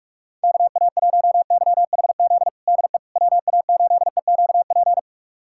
Morse code, heard as a keyed tone, about 41 wpm, 690 Hz. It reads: KA1YHZ DE WA8EQP
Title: USA Callsign DE Callsign - 6 Characters - 45wpm